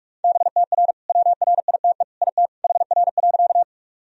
Morse code, heard as a keyed tone, about 46 wpm, 700 Hz. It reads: BTR WRITE IT HR.
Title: Sentences from Top 400 Words and Abbreviations 45wpm